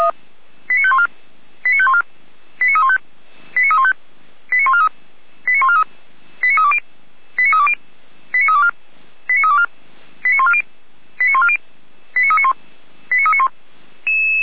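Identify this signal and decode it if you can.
Five-tone selective call (ZVEI), heard as quick runs of five tones, tones of about 70 ms. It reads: 89514 89514 89315 89315 89143 89143 89320 89320 89324 89324 89140 89140 89391 89391
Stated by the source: ZVEI five-tone selective call received on SDR, recorded by feuerwehr-orlamuende.org fire brigade.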